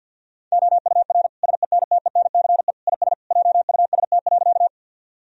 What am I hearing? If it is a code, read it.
Morse code, heard as a keyed tone, about 52 wpm, 700 Hz. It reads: OUR SENTENCE IS JUST.
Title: Sentences from Top 1000 Words - Review Part 1 - 50wpm